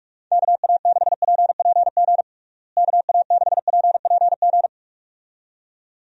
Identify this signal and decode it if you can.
Morse code, heard as a keyed tone, about 42 wpm, 700 Hz. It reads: KA6PPG KA6PPG
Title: USA Call Signs - 6 Characters - Repeated 2x - 45wpm